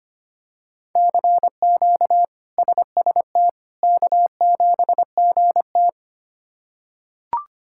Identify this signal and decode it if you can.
Morse code, heard as a keyed tone, about 25 wpm, 700 Hz. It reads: CQ SST K7GT
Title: K1USN SST QSO Practice - 25wpm